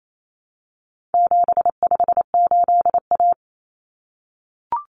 Morse code, heard as a keyed tone, about 28 wpm, 700 Hz. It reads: 758A
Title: Common Cut Numbers - 4-Digit 28wpm